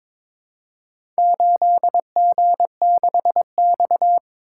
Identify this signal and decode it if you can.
Morse code, heard as a keyed tone, about 22 wpm, 700 Hz. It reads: 8G6X
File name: practice / 22wpm / Four Letter-Number Combinations - 22wpm